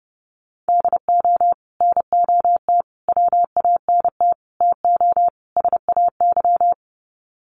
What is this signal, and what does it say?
Morse code, heard as a keyed tone, about 30 wpm, 700 Hz. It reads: DO NOT WANT TO SAY
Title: Sets of 5 Words Encore 30wpm